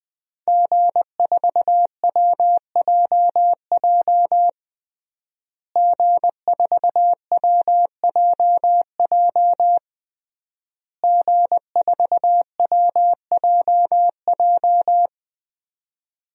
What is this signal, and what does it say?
Morse code, heard as a keyed tone, about 20 wpm, 700 Hz. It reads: G4WJJ G4WJJ G4WJJ
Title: DXCC-100 Call Signs - 5 Characters - Repeated 3x - 20wpm